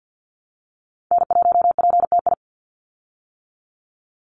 Morse code, heard as a keyed tone, about 48 wpm, 700 Hz. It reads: N1PTI
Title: USA Call Signs 50wpm